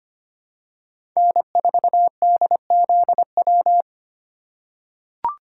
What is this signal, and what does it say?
Morse code, heard as a keyed tone, about 25 wpm, 700 Hz. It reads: N4DZW
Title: USA Call Signs - 3x Character Spacing - 25wpm